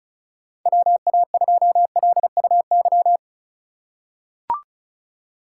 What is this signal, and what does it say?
Morse code, heard as a keyed tone, about 36 wpm, 690 Hz. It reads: WA2LUY
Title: USA Call Signs - 6 Characters - 3x Character Spacing - 35wpm